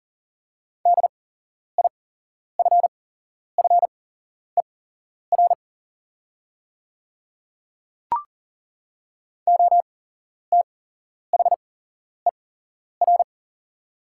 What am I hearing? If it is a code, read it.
Morse code, heard as a keyed tone, about 39 wpm, spaced as for 11 wpm, 690 Hz. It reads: DIFFER OTHER
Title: 8x Character Spacing - Top 200 Words - 40wpm